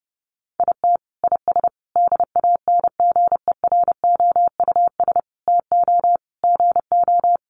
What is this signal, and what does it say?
Morse code, heard as a keyed tone, about 28 wpm, 700 Hz. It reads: IT IS DANGEROUS TO GO
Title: Sentences from Top 1200 Words 30wpm